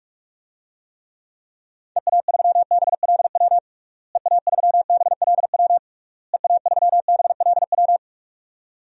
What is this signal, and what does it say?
Morse code, heard as a keyed tone, about 46 wpm, 700 Hz. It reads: EA3BLW EA3BLW EA3BLW
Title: DXCC-100 Call Signs - 6 Characters - Repeated 3x - 45wpm